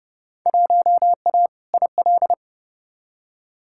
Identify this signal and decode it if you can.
Morse code, heard as a keyed tone, about 30 wpm, 700 Hz. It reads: 1A IL
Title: Field Day Exchanges Encore - 30wpm